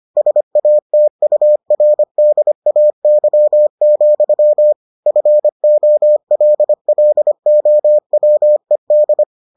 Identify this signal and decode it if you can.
Morse code, heard as a keyed tone, about 25 wpm, 600 Hz. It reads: SATURDAY, FOLLOWED